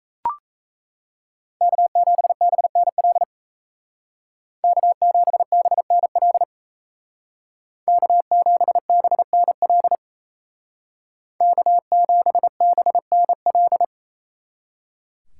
Morse code, heard as a keyed tone, about 34 wpm, 700 Hz. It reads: K7BNL K7BNL K7BNL K7BNL
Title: Speed Racing - USA 1x3 Call Signs 28wpm